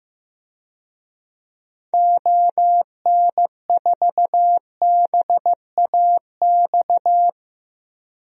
Morse code, heard as a keyed tone, about 15 wpm, 700 Hz. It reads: ON4BAX
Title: DXCC-100 Call Signs - 6 Characters - Repeated 2x - 15wpm